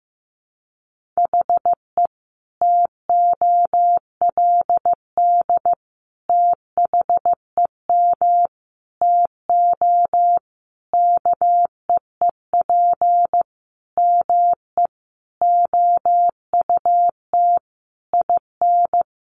Morse code, heard as a keyed tone, about 15 wpm, 700 Hz. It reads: HE TOLD THEM TO KEEP ME OUT IN
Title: Sentences from Top 1200 Words 15wpm